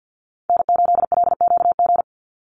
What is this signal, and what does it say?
Morse code, heard as a keyed tone, about 48 wpm, 700 Hz. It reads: N8LQG
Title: USA Call Signs 50wpm